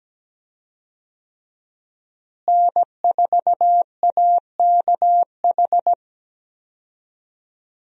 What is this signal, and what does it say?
Morse code, heard as a keyed tone, about 17 wpm, 700 Hz. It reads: N4AKH
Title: USA Call Signs - 5 Characters - Repeated 3x - 17wpm